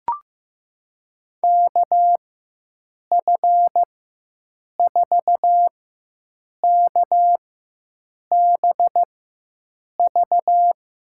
Morse code, heard as a keyed tone, about 16 wpm, spaced as for 7 wpm, 700 Hz. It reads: KF4KBV
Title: USA Call Signs - 6 Characters - 4x Character Spacing - 15wpm